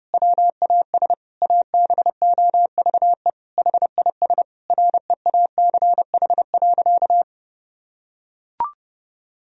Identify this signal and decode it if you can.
Morse code, heard as a keyed tone, about 30 wpm, 700 Hz. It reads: WAS ABOVE HIS REACH.
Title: Sentences from Top 1000 Words - Review Part 2 - 30wpm